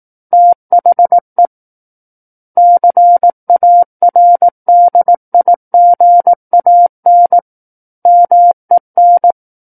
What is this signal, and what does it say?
Morse code, heard as a keyed tone, about 18 wpm, 700 Hz. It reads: THE CARDIGAN MEN
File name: Book One / 09-Chapter 9 - 18 WPM